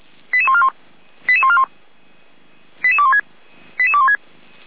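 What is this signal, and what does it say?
Five-tone selective call (ZVEI), heard as quick runs of five tones, tones of about 70 ms. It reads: 88141 88141 88216 88216
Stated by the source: ZVEI five-tone call received by SDR, from Feuerwehr Orlamünde audio server.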